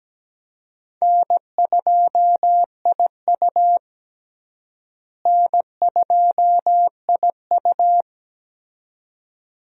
Morse code, heard as a keyed tone, about 17 wpm, 700 Hz. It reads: N2IU N2IU
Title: USA Call Signs - 4 Characters - Repeated 2x - 17wpm